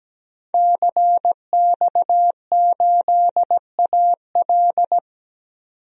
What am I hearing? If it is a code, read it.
Morse code, heard as a keyed tone, about 17 wpm, 700 Hz. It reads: CX8AL
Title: DXCC-100 Call Signs - 5 Characters - 17wpm